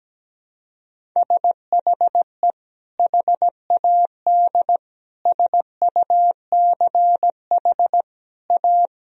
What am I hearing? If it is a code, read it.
Morse code, heard as a keyed tone, about 17 wpm, 700 Hz. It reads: SHE HAD SUCH A
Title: Sentences from Top 1000 Words - Review Part 1 - 17wpm